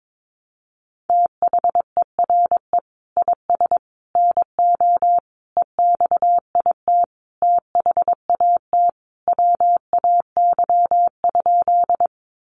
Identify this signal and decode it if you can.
Morse code, heard as a keyed tone, about 22 wpm, 700 Hz. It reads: THERE IS NO EXIT THAT WAY?
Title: Sentences from Top 1100 Words 22wpm